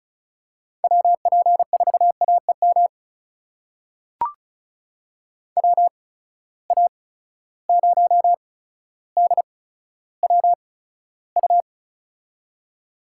Morse code, heard as a keyed tone, about 36 wpm, spaced as for 15 wpm, 700 Hz. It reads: WP4AEM WA0DWU
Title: USA Call Signs - 6 Characters - 8x Character Spacing - 35wpm